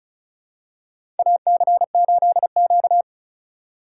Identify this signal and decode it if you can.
Morse code, heard as a keyed tone, about 36 wpm, 700 Hz. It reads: AC8Q
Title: USA Call Signs - 4 Characters - 4x Character Spacing - 35wpm